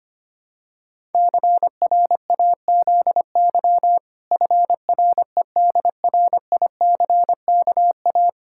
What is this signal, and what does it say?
Morse code, heard as a keyed tone, about 25 wpm, 700 Hz. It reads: CRAZY FREDRICKA
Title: Pangram Challenge v2 - 25wpm